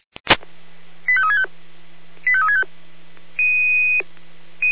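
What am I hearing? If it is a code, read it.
Five-tone selective call (ZVEI), heard as quick runs of five tones, tones of about 70 ms. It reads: 85365 85365
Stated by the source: ZVEI five-tone call received on SDR.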